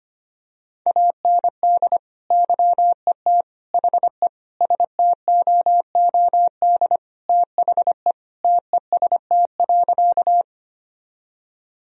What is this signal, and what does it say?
Morse code, heard as a keyed tone, about 25 wpm, 700 Hz. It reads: AND YET HE STOOD THE TEST.
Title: Sentences from Top 1000 Words - Review Part 3 - 25wpm